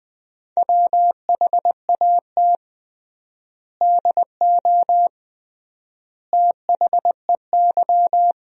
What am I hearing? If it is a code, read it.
Morse code, heard as a keyed tone, about 20 wpm, 700 Hz. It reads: WHAT DO THEY
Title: 3x Word-Spacing - Sentences from Top 100 Words - 20wpm